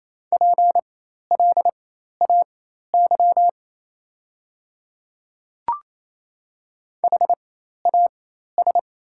Morse code, heard as a keyed tone, about 27 wpm, spaced as for 13 wpm, 700 Hz. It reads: PLAY HAS
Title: 4x Character Spacing - Top 200 Words - 28wpm